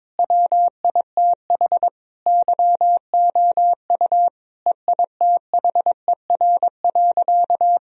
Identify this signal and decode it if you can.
Morse code, heard as a keyed tone, about 22 wpm, 700 Hz. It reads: WITH YOU EITHER.
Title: Sentences from Top 1000 Words - Review Part 4 - 22wpm